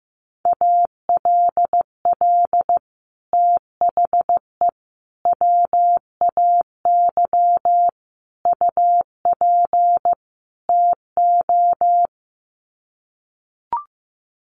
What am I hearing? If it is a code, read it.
Morse code, heard as a keyed tone, about 15 wpm, 700 Hz. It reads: ALL THE WAY UP TO
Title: Sets of 5 Words Encore 15wpm